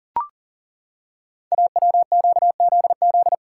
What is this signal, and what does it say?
Morse code, heard as a keyed tone, about 40 wpm, 690 Hz. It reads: AWQZZ
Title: Code Groups - 5 Letters - 40wpm